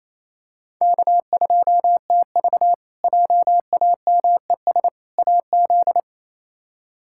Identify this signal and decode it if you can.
Morse code, heard as a keyed tone, about 27 wpm, 700 Hz. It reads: K2TV JAMES AZ
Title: K1USN SST QSO Practice - 28wpm